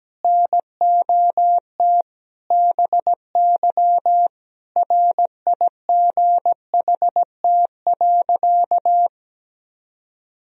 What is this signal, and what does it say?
Morse code, heard as a keyed tone, about 17 wpm, 700 Hz. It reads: NOT BY RIGHT.
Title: Sentences from Top 1000 Words - Review Part 3 - 17wpm